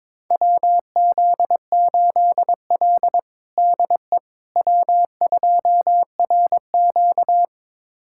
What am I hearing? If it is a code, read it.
Morse code, heard as a keyed tone, about 22 wpm, 700 Hz. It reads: WZ8L DE W2RQ
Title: USA Callsign DE Callsign - 4 Characters - 22wpm